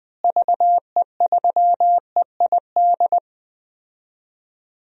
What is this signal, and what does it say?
Morse code, heard as a keyed tone, about 21 wpm, 700 Hz. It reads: VE3EID
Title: DXCC-100 Call Signs - 6 Characters - Repeated 3x - 20wpm